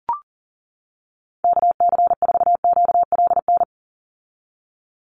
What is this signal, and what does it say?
Morse code, heard as a keyed tone, about 40 wpm, 700 Hz. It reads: KC4QLN
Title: USA Call Signs 40wpm